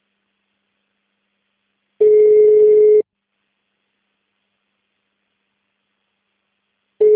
Ringback tone (Europe, 425 Hz).
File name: out- -2004- - - .9366